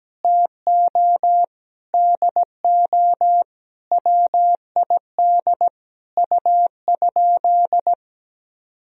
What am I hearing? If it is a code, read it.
Morse code, heard as a keyed tone, about 17 wpm, 700 Hz. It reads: TO DO WID U?
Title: Sentences from Top 100 Words and Abbreviations 17wpm